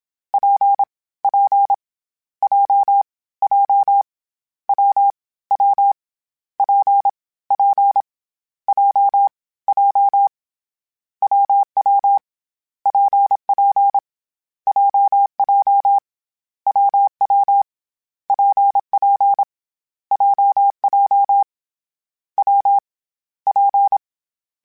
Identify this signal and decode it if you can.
Morse code, heard as a keyed tone, about 27 wpm, 800 Hz. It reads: P P J J W W P P J J WW PP JJ WW PP JJ W P
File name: Difficult Combos / Difficult Combos-mp3s / WPJ WPJ1 @25